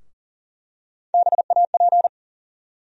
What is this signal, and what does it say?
Morse code, heard as a keyed tone, about 40 wpm, 700 Hz. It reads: BAP
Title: USA Call Sign Postfixes - 3 Letters 40wpm